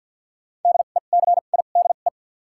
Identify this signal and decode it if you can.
Morse code, heard as a keyed tone, about 53 wpm, spaced as for 37 wpm, 700 Hz. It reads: DECIDE
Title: 2x Character Spacing - Top 500 Words - 50wpm